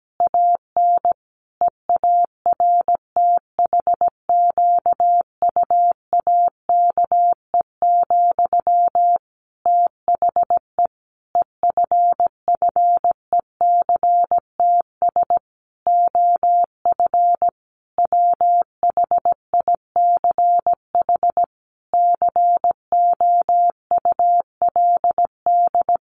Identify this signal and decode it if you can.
Morse code, heard as a keyed tone, about 17 wpm, 700 Hz. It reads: AN EARTHQUAKE, THE EFFECTS OF WHICH COULD
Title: A Sound Of Thunder 17wpm